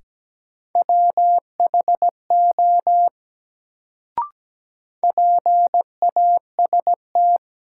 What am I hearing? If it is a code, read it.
Morse code, heard as a keyed tone, about 17 wpm, 700 Hz. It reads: WHO PAST
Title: Sets of 2 Words - New Word Drill - 17wpm